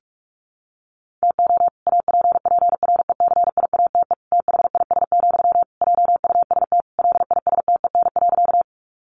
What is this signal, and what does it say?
Morse code, heard as a keyed tone, about 44 wpm, 700 Hz. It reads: TO APPRECIATE THIS, JUST LISTEN.
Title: Sentences from Top 1300 Words 45wpm